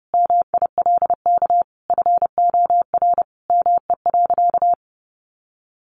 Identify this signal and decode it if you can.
Morse code, heard as a keyed tone, about 30 wpm, 700 Hz. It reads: MILK FOR ME.
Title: Sentences from Top 1500 Words 30wpm